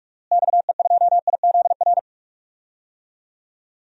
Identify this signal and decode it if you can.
Morse code, heard as a keyed tone, about 46 wpm, 700 Hz. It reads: XE2IZR
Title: DXCC-100 Call Signs - 6 Characters - Repeated 2x - 45wpm